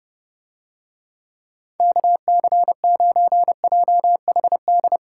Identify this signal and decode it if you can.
Morse code, heard as a keyed tone, about 29 wpm, 700 Hz. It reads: KC9JHD